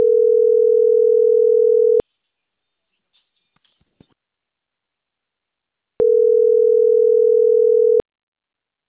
Ringback tone (North America), 440 + 480 Hz.
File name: CENTRAL CAEA / out agcaea06